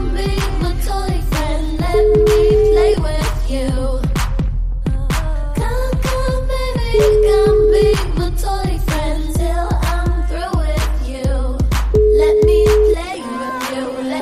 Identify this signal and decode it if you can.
Ringback tone (Europe, 425 Hz), with noise.